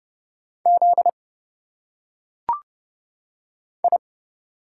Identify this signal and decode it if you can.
Morse code, heard as a keyed tone, about 29 wpm, 700 Hz. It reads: Z I
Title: Single Letters - Phonetic Pronunciation 30wpm